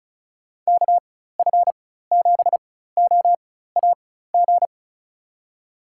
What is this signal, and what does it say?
Morse code, heard as a keyed tone, about 36 wpm, spaced as for 16 wpm, 700 Hz. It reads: KF7OAG
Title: USA Call Signs - 6 Characters - 4x Character Spacing - 35wpm